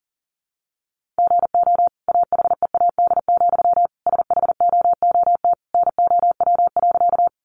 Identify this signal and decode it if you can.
Morse code, heard as a keyed tone, about 37 wpm, 700 Hz. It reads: GO AHEAD, SHOOT NOW.
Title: Sentences from Top 1200 Words 40wpm